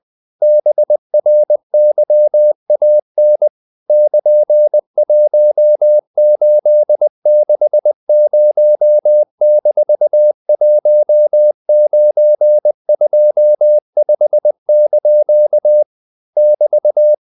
Morse code, heard as a keyed tone, about 20 wpm, 600 Hz. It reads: BRYAN <KN>1860-1925) <BT>
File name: CW / Practice / EnglishQuotations0320